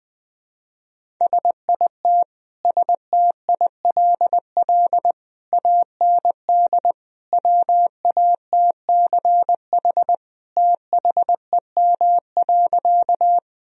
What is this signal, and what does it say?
Morse code, heard as a keyed tone, about 20 wpm, 700 Hz. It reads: SIT STILL AND WATCH THEM.
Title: Sentences from Top 1000 Words - Review Part 1 - 20wpm